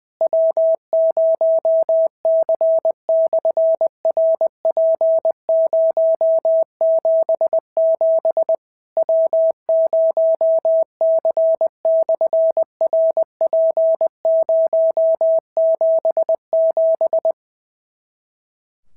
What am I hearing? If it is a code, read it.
Morse code, heard as a keyed tone, about 20 wpm, 650 Hz. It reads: W0C/RP077 W0C/RP077
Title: SOTA QSO Practice 20wpm